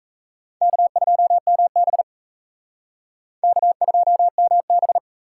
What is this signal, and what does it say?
Morse code, heard as a keyed tone, about 36 wpm, 700 Hz. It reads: K2MB K2MB